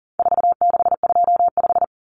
Morse code, heard as a keyed tone, about 37 wpm, 700 Hz. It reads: 4625
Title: Common Cut Numbers - 4-Digit 40wpm